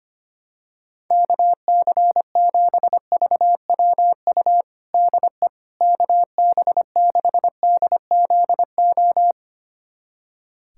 Morse code, heard as a keyed tone, about 25 wpm, 700 Hz. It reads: KC7VWU DE KB6DZO
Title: USA Callsign DE Callsign 25wpm